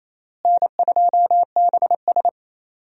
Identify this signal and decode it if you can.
Morse code, heard as a keyed tone, about 28 wpm, 700 Hz. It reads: N2BS
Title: USA Call Signs - 4 Characters - Repeated 3x - 28wpm